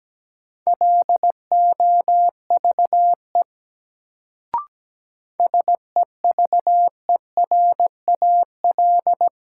Morse code, heard as a keyed tone, about 17 wpm, 700 Hz. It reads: LOVE SEVERAL
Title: Sentences from Top 400 Words - New Word Drill - 17wpm